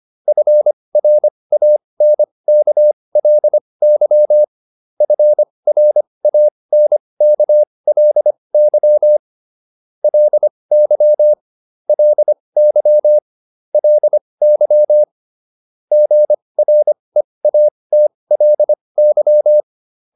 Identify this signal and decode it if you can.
Morse code, heard as a keyed tone, about 25 wpm, spaced as for 20 wpm, 600 Hz. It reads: FRANKLY FRANKLY LY LY LY GREATLY